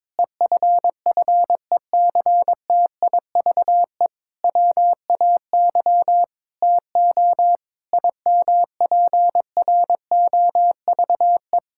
Morse code, heard as a keyed tone, about 22 wpm, 700 Hz. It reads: EFFECTIVE WAY TO IMPROVE
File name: Difficult Sentences with Hints - 22wpm